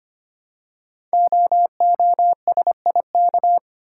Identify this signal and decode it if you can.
Morse code, heard as a keyed tone, about 25 wpm, 700 Hz. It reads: OOSIK
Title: Code Groups - 5 Letters - 25wpm